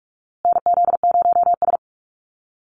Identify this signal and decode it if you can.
Morse code, heard as a keyed tone, about 44 wpm, 700 Hz. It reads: NZ0S
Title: USA Call Signs Repeated 3x 45wpm